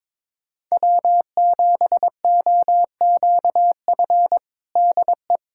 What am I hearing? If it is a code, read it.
Morse code, heard as a keyed tone, about 22 wpm, 700 Hz. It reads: W7OQF DE
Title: USA Callsign DE Callsign - 5 Characters - 22wpm